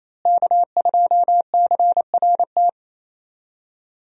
Morse code, heard as a keyed tone, about 28 wpm, 700 Hz. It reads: K2CRT
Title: USA Call Signs - 5 Characters - Repeated 3x - 28wpm